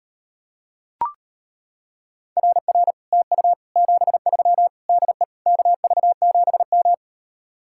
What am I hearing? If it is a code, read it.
Morse code, heard as a keyed tone, about 31 wpm, 700 Hz. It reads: RR TU 73 DE KV7M
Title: Speed-Racing Common QSO Phrases - 25wpm